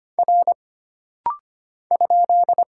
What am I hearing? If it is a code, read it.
Morse code, heard as a keyed tone, about 25 wpm, 700 Hz. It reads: R ?
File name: Single Character (Letter:Number:Punctuation:Prosign) - Rapid-Fire - 25wpm